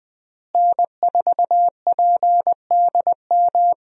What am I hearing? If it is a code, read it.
Morse code, heard as a keyed tone, about 20 wpm, 700 Hz. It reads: N4PDM
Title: USA Call Signs - 5 Characters - Repeated 2x - 20wpm